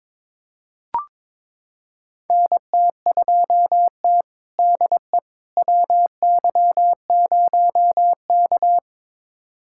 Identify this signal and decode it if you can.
Morse code, heard as a keyed tone, about 21 wpm, 700 Hz. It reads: NT2T DE WY0K
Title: USA Callsign DE Callsign - 4 Characters - 22wpm